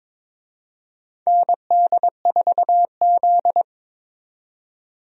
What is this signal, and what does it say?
Morse code, heard as a keyed tone, about 22 wpm, 700 Hz. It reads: ND4Z
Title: USA Call Signs - 4 Characters - 22wpm